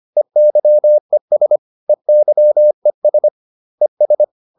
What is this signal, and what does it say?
Morse code, heard as a keyed tone, about 25 wpm, 600 Hz. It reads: EYES EYES ES